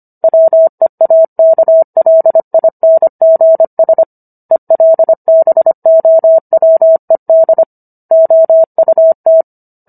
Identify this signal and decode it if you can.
Morse code, heard as a keyed tone, about 25 wpm, 650 Hz. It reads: WEAKLINGS ELBOWED OUT